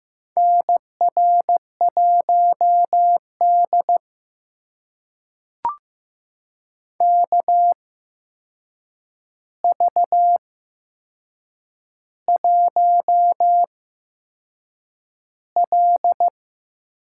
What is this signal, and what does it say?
Morse code, heard as a keyed tone, about 15 wpm, spaced as for 6 wpm, 700 Hz. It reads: NR1D KV1L
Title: USA Call Signs - 4 Characters - 8x Character Spacing - 15wpm